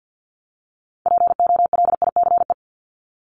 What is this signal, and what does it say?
Morse code, heard as a keyed tone, about 50 wpm, 700 Hz. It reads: POLICE
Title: Top 1751-2000 Words 50wpm